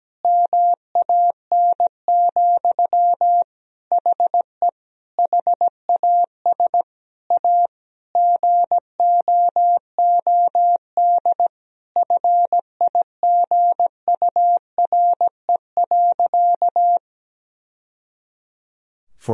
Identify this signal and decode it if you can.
Morse code, heard as a keyed tone, about 17 wpm, 700 Hz. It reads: MAN, HE HAS A GOOD FIGURE.
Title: Sentences from Top 1000 Words - Review Part 2 - 17wpm